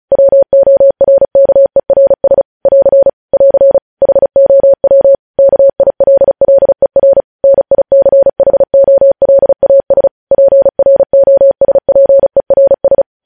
Morse code, heard as a keyed tone, about 36 wpm, 550 Hz. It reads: WORKERS <AR> <AR> HOW KILLER NICHOLAS PROSPERS
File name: podcast / News headlines-1x-35x35